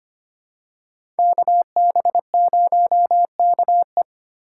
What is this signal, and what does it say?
Morse code, heard as a keyed tone, about 25 wpm, 700 Hz. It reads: KB0KE
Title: USA Call Signs - 5 Characters - 4x Character Spacing - 25wpm